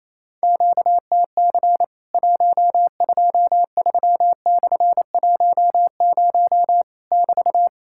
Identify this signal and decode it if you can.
Morse code, heard as a keyed tone, about 28 wpm, 700 Hz. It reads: QTC 123/10 <BT>
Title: Worked All Europe DX Contest - QTC Exchange - 28wpm